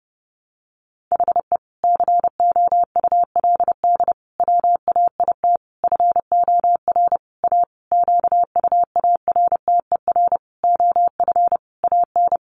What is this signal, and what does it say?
Morse code, heard as a keyed tone, about 30 wpm, 700 Hz. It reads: HE COULD WAIT FOR A QUARTER OF AN
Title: Sentences from Top 1300 Words 30wpm